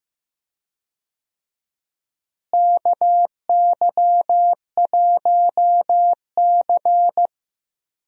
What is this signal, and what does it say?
Morse code, heard as a keyed tone, about 15 wpm, 700 Hz. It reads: KY1C
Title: USA Call Signs - 4 Characters - Repeated 2x - 15wpm